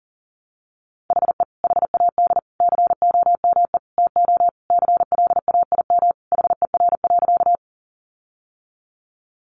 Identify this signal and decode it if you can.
Morse code, heard as a keyed tone, about 40 wpm, 700 Hz. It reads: HE HAD COME TO CLAIM HER.
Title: Sentences from Top 1500 Words 40wpm